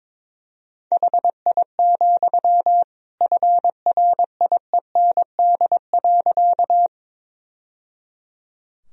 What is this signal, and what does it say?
Morse code, heard as a keyed tone, about 21 wpm, 700 Hz. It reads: HI, FRIEND.
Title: Sentences from Top 1000 Words - Review Part 3 - 22wpm